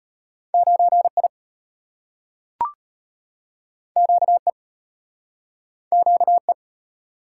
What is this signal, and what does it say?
Morse code, heard as a keyed tone, about 34 wpm, 700 Hz. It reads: 9I QE QE